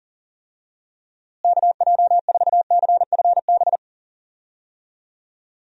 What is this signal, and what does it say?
Morse code, heard as a keyed tone, about 40 wpm, 700 Hz. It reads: KJ4CFB
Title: USA Call Signs - 6 Characters - Repeated 2x - 40wpm